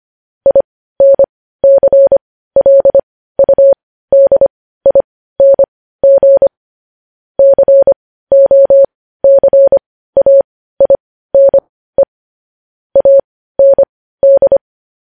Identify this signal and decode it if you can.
Morse code, heard as a keyed tone, about 25 wpm, spaced as for 15 wpm, 550 Hz. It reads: INCLUDING COCAINE AND
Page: news headlines in morse code at 15 wpm